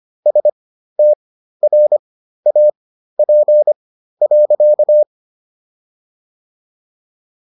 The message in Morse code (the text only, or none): STRAP.